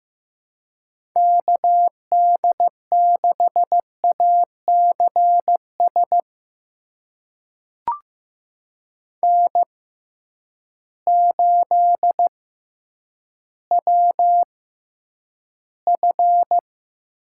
Morse code, heard as a keyed tone, about 15 wpm, spaced as for 9 wpm, 700 Hz. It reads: KD6ACS N8WF
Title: USA Call Signs - 6x Character Spacing - 15wpm